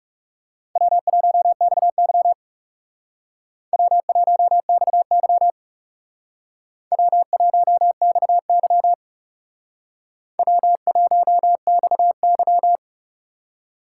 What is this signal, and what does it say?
Morse code, heard as a keyed tone, about 36 wpm, 700 Hz. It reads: W1XY W1XY W1XY W1XY
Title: Speed Racing - USA 1x2 Call Signs 30wpm